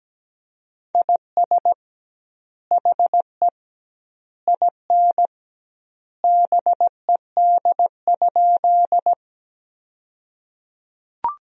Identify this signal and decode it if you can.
Morse code, heard as a keyed tone, about 17 wpm, 700 Hz. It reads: IS HE IN BED?
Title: 2x Word-Spacing - Sentences from Top 500 Words - 17wpm